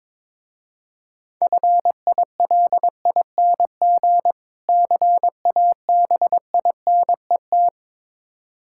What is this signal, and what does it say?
Morse code, heard as a keyed tone, about 23 wpm, 700 Hz. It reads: FILING CABINET
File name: Easy Words - Speed-Racing with Hints - 22wpm